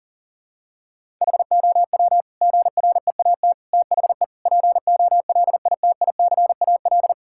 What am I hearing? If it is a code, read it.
Morse code, heard as a keyed tone, about 40 wpm, 700 Hz. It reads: HOW GREAT THE POLITICAL
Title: Sentences from Top 1000 Words - Review Part 5 - 40wpm